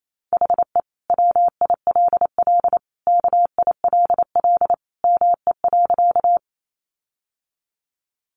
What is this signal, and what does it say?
Morse code, heard as a keyed tone, about 28 wpm, 700 Hz. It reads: HE WILL KILL ME.
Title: Sentences from Top 1400 Words 28wpm